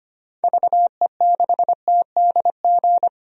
Morse code, heard as a keyed tone, about 25 wpm, 700 Hz. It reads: VE6TDG
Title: DXCC-100 Call Signs - 6 Characters - 25wpm